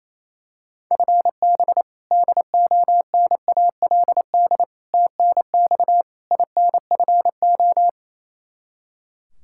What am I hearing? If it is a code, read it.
Morse code, heard as a keyed tone, about 27 wpm, 700 Hz. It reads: FB DONALD TNX INFO
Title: Practice QSO - Exchange 3 - More Info - 28wpm